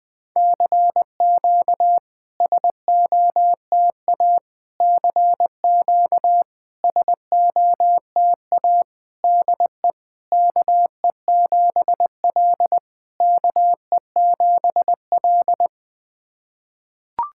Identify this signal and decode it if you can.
Morse code, heard as a keyed tone, about 20 wpm, 690 Hz. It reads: CQ SOTA CQ SOTA DE KE7L KE7L
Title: SOTA QSO Practice 20wpm